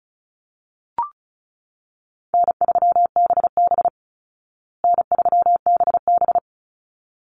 Morse code, heard as a keyed tone, about 34 wpm, 700 Hz. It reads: N3BB N3BB
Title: Easy Call Signs Repeated 2x - 35wpm